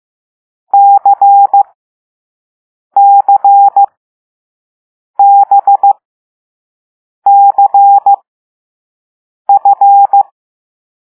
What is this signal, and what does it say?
Morse code, heard as a keyed tone, about 15 wpm, spaced as for 5 wpm, 800 Hz. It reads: CCBCF
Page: Play Lesson 6 C - all letters in previous lessons